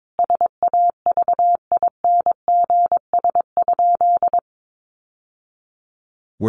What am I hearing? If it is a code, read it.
Morse code, heard as a keyed tone, about 22 wpm, 700 Hz. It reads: SAVINGS?
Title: Sentences from Top 1200 Words 22wpm